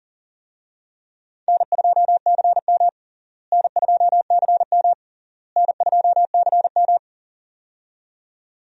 Morse code, heard as a keyed tone, about 40 wpm, 700 Hz. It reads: N2CM N2CM N2CM
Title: USA Call Signs - 4 Characters - Repeated 3x - 40wpm